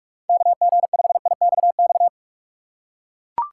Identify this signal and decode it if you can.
Morse code, heard as a keyed tone, about 41 wpm, 700 Hz. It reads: KG5IXX
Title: USA Call Signs - 6 Characters - 6x Character Spacing - 45wpm